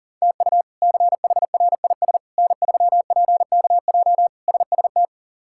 Morse code, heard as a keyed tone, about 40 wpm, 700 Hz. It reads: TU CHRIS N3PKJ SST
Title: K1USN SST QSO Practice - 40wpm